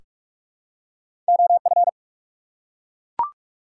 Morse code, heard as a keyed tone, about 44 wpm, 700 Hz. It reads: OF
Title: Two Letter Words - 45wpm